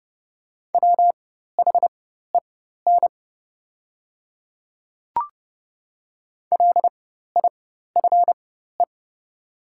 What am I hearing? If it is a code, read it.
Morse code, heard as a keyed tone, about 31 wpm, spaced as for 14 wpm, 690 Hz. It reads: WHEN LIFE
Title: 4x Character Spacing - Top 500 Words - 30wpm